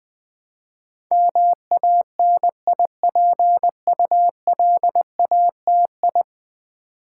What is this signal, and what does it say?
Morse code, heard as a keyed tone, about 20 wpm, 700 Hz. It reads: MANIPULATI
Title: Build A Word - 5 Letters At A Time - 12-Letter Words - 20wpm